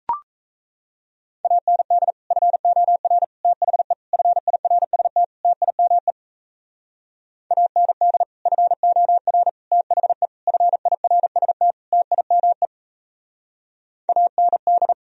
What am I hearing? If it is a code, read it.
Morse code, heard as a keyed tone, about 34 wpm, 700 Hz. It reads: AND FOR THE FIRST TIME AND FOR THE FIRST TIME AND
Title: Speed-Racing - Sets of 5 Words 28wpm